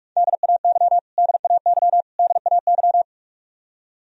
Morse code, heard as a keyed tone, about 46 wpm, 700 Hz. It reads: DAY DAY DAY
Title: Top 100 Words Repeated 45wpm